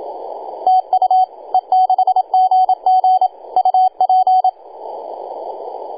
Morse code, heard as a keyed tone, about 27 wpm, 750 Hz, with radio noise. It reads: TU E6GG UP
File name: e6gg-18-083-cw